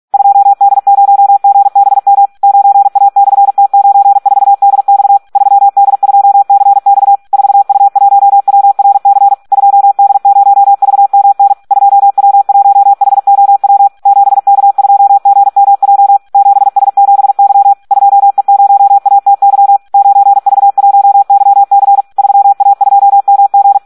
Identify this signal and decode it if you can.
Morse code, heard as a keyed tone, about 46 wpm, 800 Hz. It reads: 2D0GBM 9A/T94DX 3D2CX 4U1WRC 2D0VMN 2W1HOW 7K2GMJ 7S7Y 2E0ATY 9V1YC 3A2MG
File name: Audio practica CW / SD 1 46